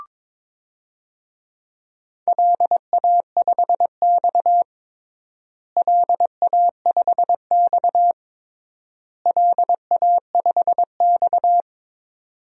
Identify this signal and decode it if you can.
Morse code, heard as a keyed tone, about 22 wpm, 700 Hz. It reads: LA5X LA5X LA5X
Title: DXCC-100 Call Signs - 4 Characters - Repeated 3x - 22wpm